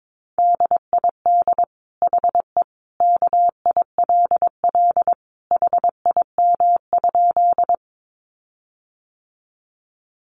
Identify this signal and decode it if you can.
Morse code, heard as a keyed tone, about 22 wpm, 700 Hz. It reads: DID HE KILL HIM?
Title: Sentences from Top 1400 Words 22wpm